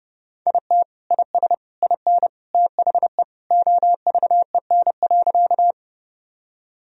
Morse code, heard as a keyed tone, about 31 wpm, 700 Hz. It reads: IT IS IN THE OVEN.
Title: Sentences from Top 1000 Words - Review Part 4 - 30wpm